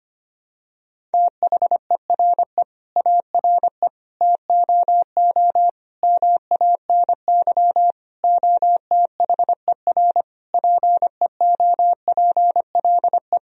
Morse code, heard as a keyed tone, about 25 wpm, 700 Hz. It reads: THERE ARE TOO MANY OTHER PEOPLE
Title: Sentences from Top 1000 Words - Review Part 3 - 25wpm